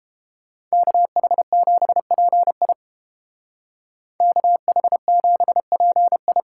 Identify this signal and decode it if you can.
Morse code, heard as a keyed tone, about 30 wpm, 700 Hz. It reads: KH7PI KH7PI